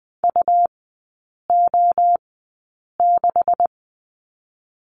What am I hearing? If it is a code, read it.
Morse code, heard as a keyed tone, about 20 wpm, spaced as for 8 wpm, 700 Hz. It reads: UO6
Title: 152 Lesson 25e - Letter p - Three Characters 20wpm